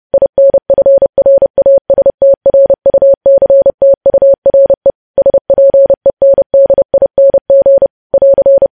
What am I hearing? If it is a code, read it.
Morse code, heard as a keyed tone, about 30 wpm, 550 Hz. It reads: INFRASTRUCTURE SPENDING <AR>
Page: news headlines in morse code at 30 wpm